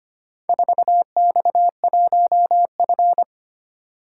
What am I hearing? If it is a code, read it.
Morse code, heard as a keyed tone, about 25 wpm, 700 Hz. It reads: 4X1F
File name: practice / 25wpm / Four Letter-Number Combinations - 25wpm